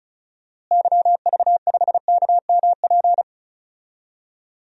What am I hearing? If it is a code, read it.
Morse code, heard as a keyed tone, about 36 wpm, 700 Hz. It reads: YV5KMP
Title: DXCC-100 Call Signs - 6 Characters - Repeated 2x - 35wpm